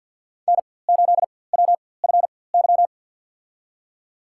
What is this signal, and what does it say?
Morse code, heard as a keyed tone, about 47 wpm, spaced as for 23 wpm, 700 Hz. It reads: N8WFY
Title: USA Call Signs - 5 Characters - 4x Character Spacing - 50wpm